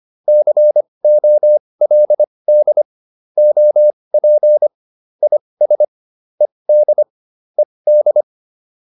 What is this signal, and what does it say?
Morse code, heard as a keyed tone, about 25 wpm, spaced as for 20 wpm, 600 Hz. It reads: COLD OP IS ED ED